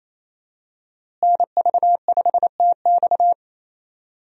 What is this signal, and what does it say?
Morse code, heard as a keyed tone, about 28 wpm, 700 Hz. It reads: NV5TX
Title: USA Call Signs - 5 Characters - Repeated 2x - 28wpm